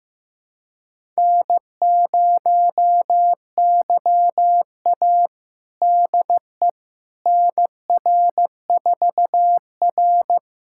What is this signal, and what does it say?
Morse code, heard as a keyed tone, about 15 wpm, 700 Hz. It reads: N0YA DE NR4R
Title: USA Callsign DE Callsign - 4 Characters - 15wpm